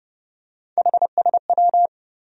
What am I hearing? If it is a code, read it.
Morse code, heard as a keyed tone, about 30 wpm, 700 Hz. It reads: HSW
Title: USA Call Sign Postfixes - 3 Letters 30wpm